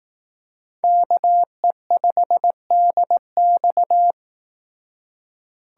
Morse code, heard as a keyed tone, about 18 wpm, 700 Hz. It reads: KE5DX
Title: Speed Racing - USA 2x2 Call Signs 15wpm